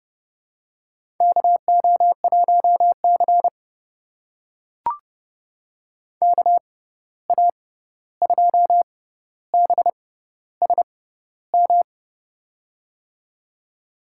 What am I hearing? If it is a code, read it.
Morse code, heard as a keyed tone, about 32 wpm, spaced as for 14 wpm, 700 Hz. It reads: KO1C KA2BSM
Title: USA Call Signs - 6x Character Spacing - 30wpm